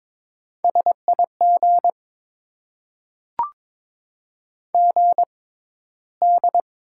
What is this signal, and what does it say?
Morse code, heard as a keyed tone, about 22 wpm, spaced as for 13 wpm, 700 Hz. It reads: SIG GD
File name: Top 90 QSO Elements - 6x Character Spacing - 22wpm